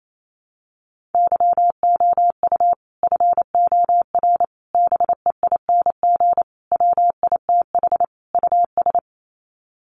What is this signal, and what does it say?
Morse code, heard as a keyed tone, about 28 wpm, 700 Hz. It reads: YOU FOR BEING WITH US
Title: Sets of 5 Words 28wpm